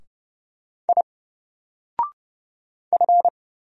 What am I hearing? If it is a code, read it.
Morse code, heard as a keyed tone, about 30 wpm, 700 Hz. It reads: I F
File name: Single-Letter-Number-30wpm